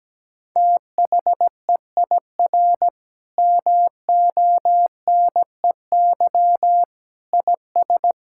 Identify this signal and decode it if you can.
Morse code, heard as a keyed tone, about 17 wpm, 700 Hz. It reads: THEIR MONEY IS
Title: Sentences from Top 1600 Words 17wpm